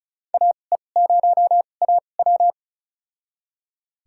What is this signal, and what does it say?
Morse code, heard as a keyed tone, about 34 wpm, spaced as for 25 wpm, 700 Hz. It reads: AE0AW
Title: USA Call Signs - 5 Characters - 2x Character Spacing - 35wpm